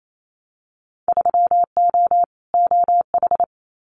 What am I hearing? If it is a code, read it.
Morse code, heard as a keyed tone, about 28 wpm, 700 Hz. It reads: 3O OH
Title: Winter Field Day Exchanges 28wpm